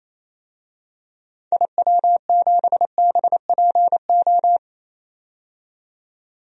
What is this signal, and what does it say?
Morse code, heard as a keyed tone, about 28 wpm, 700 Hz. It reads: IW7BPO
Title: DXCC-100 Call Signs - 6 Characters - 28wpm